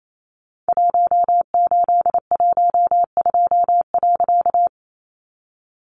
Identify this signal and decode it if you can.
Morse code, heard as a keyed tone, about 28 wpm, 700 Hz. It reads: 1812.
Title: A Sound Of Thunder 28wpm